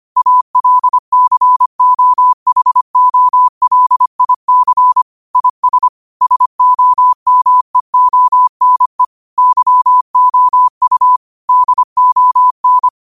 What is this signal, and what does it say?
Morse code, heard as a keyed tone, about 25 wpm, 1000 Hz. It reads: ALCOHOLIC IS SOMEONE YOU DON
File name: qotdfiles / 25wpm